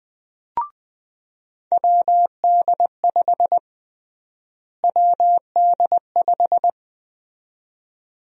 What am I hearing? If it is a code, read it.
Morse code, heard as a keyed tone, about 20 wpm, 700 Hz. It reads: WD5 WD5
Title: USA Call Sign Prefixes - Repeated 2x - 20wpm